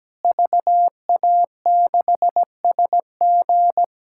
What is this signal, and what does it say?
Morse code, heard as a keyed tone, about 17 wpm, 700 Hz. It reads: VA6SG
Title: K1USN SST QSO Practice - 17wpm